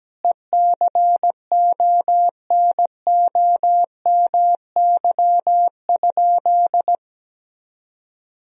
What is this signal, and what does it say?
Morse code, heard as a keyed tone, about 17 wpm, 700 Hz. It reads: ECONOMY?
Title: Sentences from Top 1000 Words - Review Part 3 - 17wpm